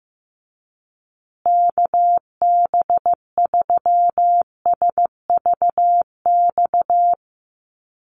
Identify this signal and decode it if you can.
Morse code, heard as a keyed tone, about 14 wpm, 700 Hz. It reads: KB3SVX
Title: USA Call Signs Repeated 2x 15wpm